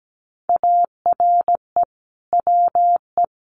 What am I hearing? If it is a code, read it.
Morse code, heard as a keyed tone, about 17 wpm, 700 Hz. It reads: ARE WE
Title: Sentences from Top 1300 Words 17wpm